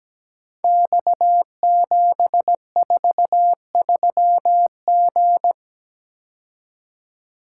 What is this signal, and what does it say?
Morse code, heard as a keyed tone, about 17 wpm, 700 Hz. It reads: X743G
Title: Code Groups - 5 Letter:Numbers - 17wpm